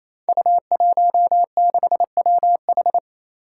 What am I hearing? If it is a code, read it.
Morse code, heard as a keyed tone, about 27 wpm, 700 Hz. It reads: U16WH
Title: Five Challenging Letter-Number Combinations 28wpm